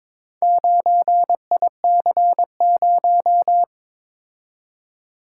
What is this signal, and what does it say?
Morse code, heard as a keyed tone, about 22 wpm, 700 Hz. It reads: 9IC0
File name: practice / 22wpm / Four Letter-Number Combinations - 22wpm